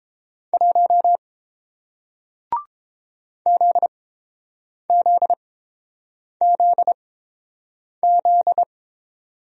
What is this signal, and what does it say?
Morse code, heard as a keyed tone, about 30 wpm, spaced as for 7 wpm, 700 Hz. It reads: 1 ZZZZ